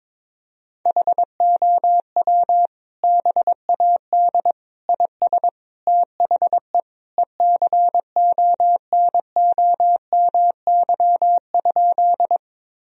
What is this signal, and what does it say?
Morse code, heard as a keyed tone, about 22 wpm, 700 Hz. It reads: HOW BAD IS THE ECONOMY?
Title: Sentences from Top 1000 Words - Review Part 3 - 22wpm